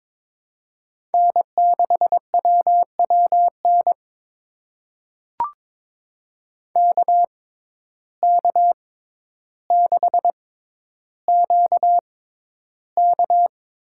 Morse code, heard as a keyed tone, about 21 wpm, spaced as for 11 wpm, 700 Hz. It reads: N6WWN KK6QK
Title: USA Call Signs - 6x Character Spacing - 22wpm